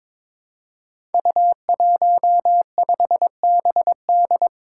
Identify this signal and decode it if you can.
Morse code, heard as a keyed tone, about 23 wpm, 700 Hz. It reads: U15BD
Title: Five Challenging Letter-Number Combinations 22wpm